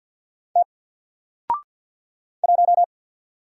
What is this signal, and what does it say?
Morse code, heard as a keyed tone, about 48 wpm, 700 Hz. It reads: T 1
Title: Single Letter-Number Alt 50wpm